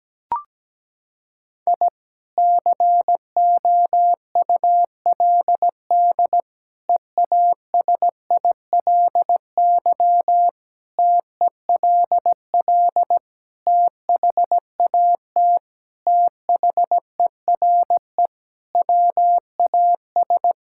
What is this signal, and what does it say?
Morse code, heard as a keyed tone, about 17 wpm, 700 Hz. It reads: I COULD EASILY TELL THAT THERE WAS
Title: Sentences from Top 1000 Words - Review Part 6 - 17wpm